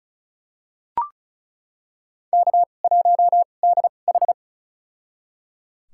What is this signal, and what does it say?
Morse code, heard as a keyed tone, about 36 wpm, spaced as for 25 wpm, 700 Hz. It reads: K1DH
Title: USA Call Signs - 2x Character Spacing - 35wpm